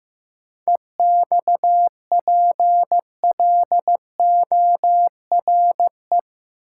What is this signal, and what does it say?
Morse code, heard as a keyed tone, about 15 wpm, 700 Hz. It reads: EXPLORE
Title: Sentences from Top 1100 Words - New Word Drill Alt - 15wpm